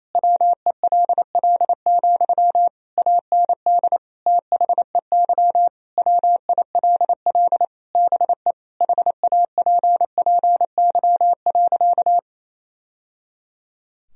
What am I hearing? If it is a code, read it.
Morse code, heard as a keyed tone, about 28 wpm, 700 Hz. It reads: WELL, AND THEY WILL BE HAPPY.
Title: Sentences from Top 1000 Words - Review Part 2 - 28wpm